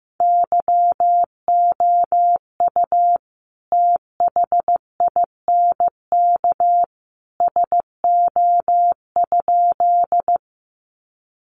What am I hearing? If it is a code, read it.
Morse code, heard as a keyed tone, about 15 wpm, 700 Hz. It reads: YOU THINK SO?
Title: Sentences from Top 200 Words 15wpm